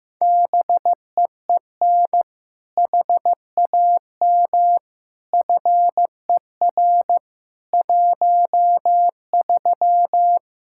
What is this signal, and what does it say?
Morse code, heard as a keyed tone, about 15 wpm, 700 Hz. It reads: BEEN HAM FER 13
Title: Practice QSO - Exchange 3 - More Info - 15wpm